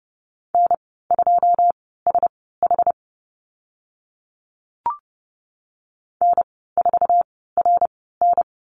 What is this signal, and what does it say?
Morse code, heard as a keyed tone, about 30 wpm, spaced as for 17 wpm, 700 Hz. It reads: N2SH N4RN
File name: Lesson 14g - Number 2 - Call Signs 30wpm